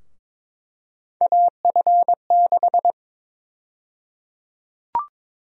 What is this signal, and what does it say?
Morse code, heard as a keyed tone, about 22 wpm, 700 Hz. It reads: AF6
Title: USA Call Sign Prefixes - Repeated 2x - 22wpm